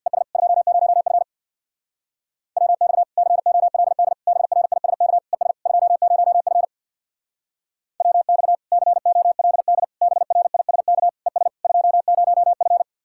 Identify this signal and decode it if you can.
Morse code, heard as a keyed tone, about 62 wpm, 700 Hz. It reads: ES 20F WX COLD BRISK ES 20F WX COLD BRISK ES 20F
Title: Speed-Racing Common QSO Phrases - 50wpm